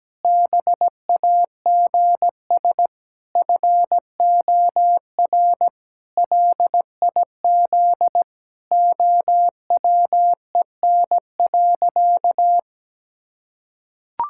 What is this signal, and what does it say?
Morse code, heard as a keyed tone, about 17 wpm, 700 Hz. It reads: BAGS FOR LIZ OWEN.
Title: Pangram Challenge v2 - 17wpm